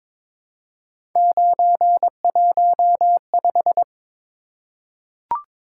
Morse code, heard as a keyed tone, about 22 wpm, 700 Hz. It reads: 915
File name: practice / 22wpm / Three Letter-Number Combinations - 22wpm